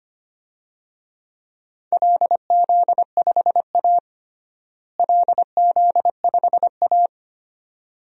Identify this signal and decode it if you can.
Morse code, heard as a keyed tone, about 25 wpm, 700 Hz. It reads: LZ5A LZ5A
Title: DXCC-100 Call Signs - 4 Characters - Repeated 2x - 25wpm